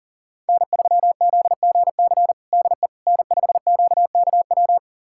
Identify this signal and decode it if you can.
Morse code, heard as a keyed tone, about 40 wpm, 700 Hz. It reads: N3ZGC DE N5QKW
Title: USA Callsign DE Callsign - 5 Characters - 40wpm